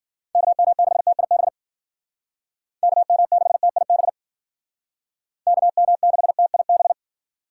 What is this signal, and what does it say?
Morse code, heard as a keyed tone, about 53 wpm, 700 Hz. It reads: KM6TIB KM6TIB KM6TIB